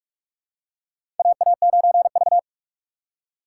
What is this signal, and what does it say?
Morse code, heard as a keyed tone, about 46 wpm, 700 Hz. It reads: AA9V
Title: USA Call Signs - 4 Characters - 4x Character Spacing - 45wpm